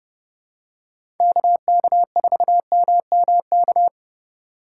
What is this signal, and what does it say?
Morse code, heard as a keyed tone, about 30 wpm, 700 Hz. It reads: KK4MMK
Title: USA Call Signs - 2x Character Spacing - 30wpm